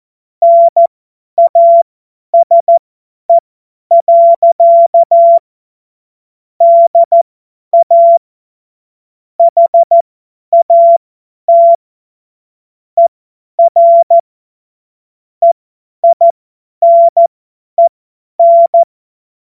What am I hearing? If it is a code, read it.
Morse code, heard as a keyed tone, about 14 wpm, spaced as for 10 wpm, 680 Hz. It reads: NASE. DA HAT ER EINEN
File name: Morsekurs / Witze / Witz 12. April 2022 50 F70